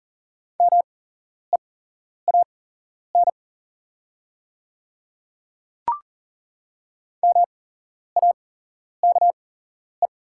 Morse code, heard as a keyed tone, about 40 wpm, spaced as for 11 wpm, 700 Hz. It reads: MEAN MAKE
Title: 8x Character Spacing - Top 200 Words - 40wpm